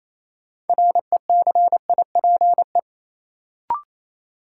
Morse code, heard as a keyed tone, about 28 wpm, 700 Hz. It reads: RECIPE
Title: Sentences from Top 1200 Words - New Word Drill - 28wpm